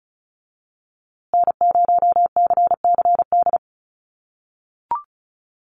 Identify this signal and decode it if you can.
Morse code, heard as a keyed tone, about 34 wpm, 700 Hz. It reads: N0CCD
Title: USA Call Signs Repeated 3x 35wpm